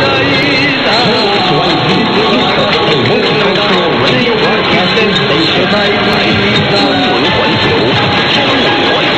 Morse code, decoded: none

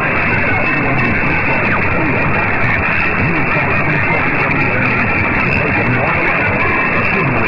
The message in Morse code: none

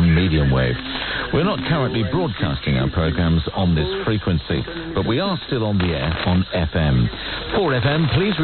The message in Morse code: none

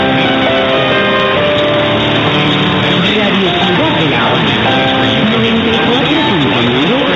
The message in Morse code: none

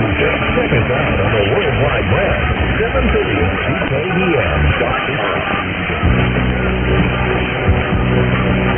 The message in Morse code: none

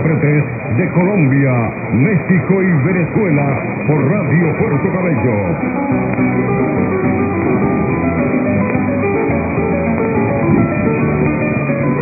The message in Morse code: none

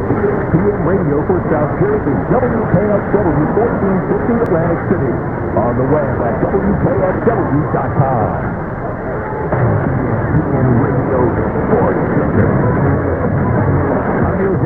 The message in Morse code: none

